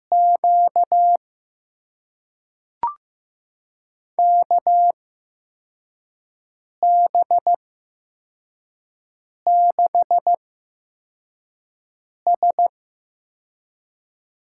Morse code, heard as a keyed tone, about 15 wpm, spaced as for 4 wpm, 700 Hz. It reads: Q KB6S
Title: USA Call Signs - 6 Characters - 8x Character Spacing - 15wpm